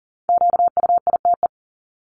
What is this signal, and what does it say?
Morse code, heard as a keyed tone, about 40 wpm, 700 Hz. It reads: QUITE
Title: Top 501-750 Words 40wpm